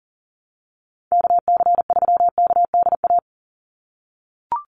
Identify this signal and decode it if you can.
Morse code, heard as a keyed tone, about 40 wpm, 700 Hz. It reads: KC3KDA
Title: USA Call Signs 40wpm